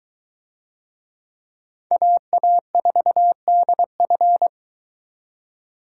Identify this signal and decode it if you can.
Morse code, heard as a keyed tone, about 21 wpm, 700 Hz. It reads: AA4DF